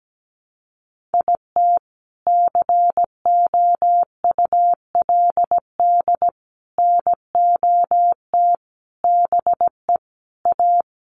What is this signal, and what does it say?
Morse code, heard as a keyed tone, about 17 wpm, 700 Hz. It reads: IT COULD NOT BE A
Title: Sentences from Top 1500 Words 17wpm